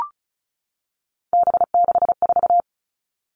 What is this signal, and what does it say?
Morse code, heard as a keyed tone, about 34 wpm, 700 Hz. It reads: B64
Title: All Cut Numbers - 3-Digit 35wpm